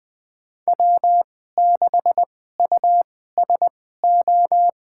Morse code, heard as a keyed tone, about 20 wpm, spaced as for 14 wpm, 700 Hz. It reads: W6USO
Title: USA Call Signs - 2x Character Spacing - 20wpm